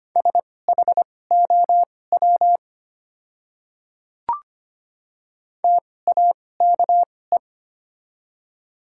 Morse code, heard as a keyed tone, about 25 wpm, spaced as for 18 wpm, 700 Hz. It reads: SHOW TAKE
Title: 2x Character Spacing - Top 200 Words - 25wpm